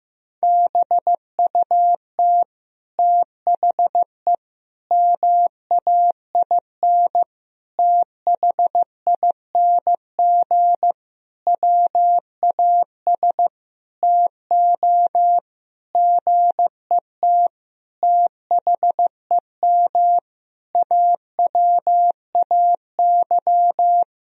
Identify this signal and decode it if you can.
Morse code, heard as a keyed tone, about 15 wpm, 700 Hz. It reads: BUT THE MAIN THING WAS TO GET THEM AWAY
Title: Sentences from Top 1000 Words - Review Part 2 - 15wpm